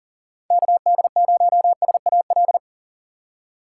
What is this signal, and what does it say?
Morse code, heard as a keyed tone, about 40 wpm, 700 Hz. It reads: KD0SAL
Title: USA Call Signs - 6 Characters - 6x Character Spacing - 40wpm